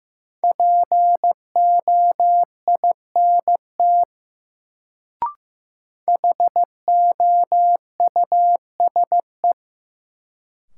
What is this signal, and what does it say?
Morse code, heard as a keyed tone, about 15 wpm, 700 Hz. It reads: POINT HOUSE
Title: Sentences from Top 200 Words - New Word Drill - 15wpm